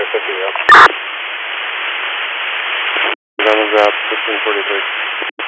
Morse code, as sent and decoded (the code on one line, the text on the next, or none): none